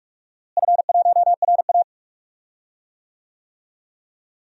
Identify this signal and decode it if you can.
Morse code, heard as a keyed tone, about 46 wpm, 700 Hz. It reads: F1RA
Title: DXCC-100 Call Signs - 4 Characters - Repeated 3x - 45wpm